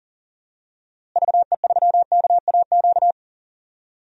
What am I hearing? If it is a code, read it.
Morse code, heard as a keyed tone, about 40 wpm, 700 Hz. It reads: VE3KAQ
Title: DXCC-100 Call Signs - 6 Characters - Repeated 2x - 40wpm